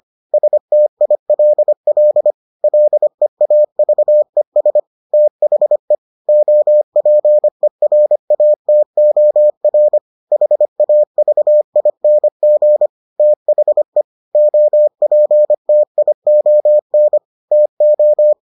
Morse code, heard as a keyed tone, about 25 wpm, 600 Hz. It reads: STILL LEAVES THE OPERATOR HAVING THE OPTION TO